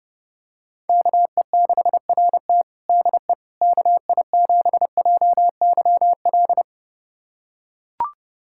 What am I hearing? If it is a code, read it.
Morse code, heard as a keyed tone, about 30 wpm, 700 Hz. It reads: KE6RT DE KI7JYL
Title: USA Callsign DE Callsign 30wpm